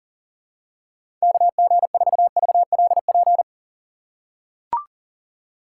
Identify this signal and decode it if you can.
Morse code, heard as a keyed tone, about 40 wpm, 700 Hz. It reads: KG4VLP
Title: USA Call Signs - 6 Characters - 4x Character Spacing - 40wpm